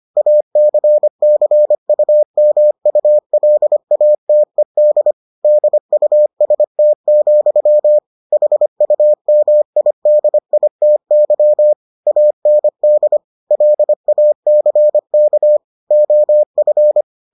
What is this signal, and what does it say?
Morse code, heard as a keyed tone, about 25 wpm, 600 Hz. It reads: ACCUMULATED DUST, HUMIDITY AND LACK OF